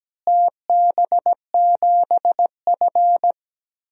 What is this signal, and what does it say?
Morse code, heard as a keyed tone, about 17 wpm, 700 Hz. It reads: TB7F
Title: DXCC-100 Call Signs - 4 Characters - Repeated 3x - 17wpm